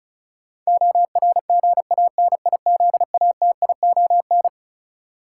Morse code, heard as a keyed tone, about 36 wpm, 700 Hz. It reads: ORGANIZATION
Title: Sentences from Top 1100 Words - New Word Drill Alt - 35wpm